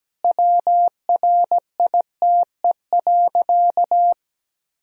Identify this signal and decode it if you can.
Morse code, heard as a keyed tone, about 17 wpm, 700 Hz. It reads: WRITE.
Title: Sentences from Top 1000 Words - Review Part 1 - 17wpm